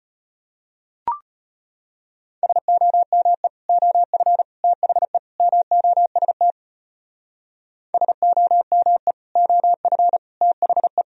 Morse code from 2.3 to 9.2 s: SOME OF THE MOST SOME

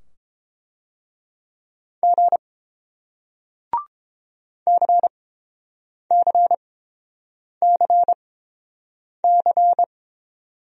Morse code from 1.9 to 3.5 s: G

